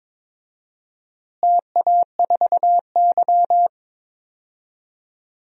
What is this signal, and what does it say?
Morse code, heard as a keyed tone, about 22 wpm, 700 Hz. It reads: TA4Y
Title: DXCC-100 Call Signs - 4 Characters - Repeated 3x - 22wpm